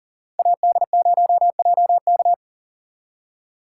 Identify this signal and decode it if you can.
Morse code, heard as a keyed tone, about 49 wpm, 700 Hz. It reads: AD0JK
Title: Speed Racing - USA 2x2 Call Signs 40wpm